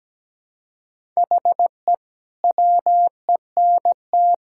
Morse code, heard as a keyed tone, about 17 wpm, 700 Hz. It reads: HE WENT
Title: Sentences from Top 1000 Words - Review Part 5 - 17wpm